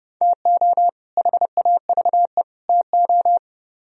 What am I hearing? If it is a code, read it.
Morse code, heard as a keyed tone, about 29 wpm, 700 Hz. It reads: TO HAVE TO